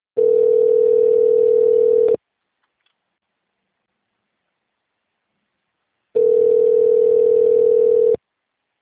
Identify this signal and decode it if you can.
Ringback tone (North America), 440 + 480 Hz.
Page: Itapetininga